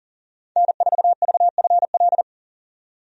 Morse code, heard as a keyed tone, about 40 wpm, 700 Hz. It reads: N4VFL
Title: USA Call Signs - 5 Characters - 40wpm